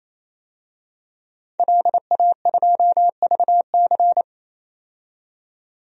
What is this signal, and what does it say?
Morse code, heard as a keyed tone, about 27 wpm, 700 Hz. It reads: LA2VC
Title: DXCC-100 Call Signs - 5 Characters - 28wpm